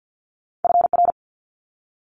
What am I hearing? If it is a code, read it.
Morse code, heard as a keyed tone, about 50 wpm, 700 Hz. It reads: FR
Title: Multi-letter Phonemes Quick Fire 50wpm